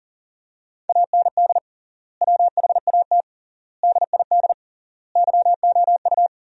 Morse code, heard as a keyed tone, about 39 wpm, 700 Hz. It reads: AND WHAT DID YOU
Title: 3x Word-Spacing - Sentences from Top 300 Words - 40wpm